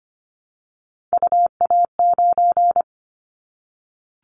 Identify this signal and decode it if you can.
Morse code, heard as a keyed tone, about 25 wpm, 700 Hz. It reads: UA9
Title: Callsign Prefixes for Top 25 DXCC - Repeated 3x - 25wpm